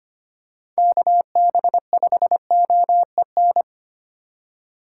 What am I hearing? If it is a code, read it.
Morse code, heard as a keyed tone, about 25 wpm, 700 Hz. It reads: KB5OEN
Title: USA Call Signs - 2x Character Spacing - 25wpm